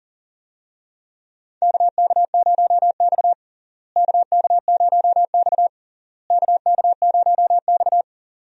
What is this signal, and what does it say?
Morse code, heard as a keyed tone, about 40 wpm, 700 Hz. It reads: KK0X KK0X KK0X
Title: USA Call Signs - 4 Characters - Repeated 3x - 40wpm